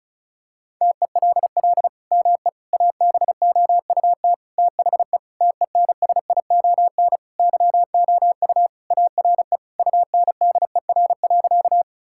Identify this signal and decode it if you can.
Morse code, heard as a keyed tone, about 36 wpm, 700 Hz. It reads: TELL ME ABOUT THE TENSION YOU ARE UNDER.
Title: Sentences from Top 1600 Words 35wpm